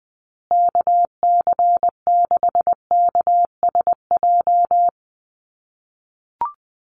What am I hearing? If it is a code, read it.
Morse code, heard as a keyed tone, about 20 wpm, 700 Hz. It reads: KC6KSJ
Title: USA Call Signs Repeated 3x 20wpm